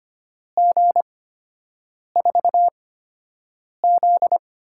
Morse code, heard as a keyed tone, about 25 wpm, spaced as for 7 wpm, 700 Hz. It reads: G4Z
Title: USA Call Signs - 6 Characters - 8x Character Spacing - 25wpm